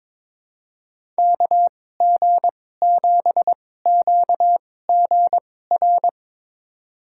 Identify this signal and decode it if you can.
Morse code, heard as a keyed tone, about 22 wpm, spaced as for 16 wpm, 700 Hz. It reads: KG7QGR
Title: USA Call Signs - 6 Characters - 2x Character Spacing - 22wpm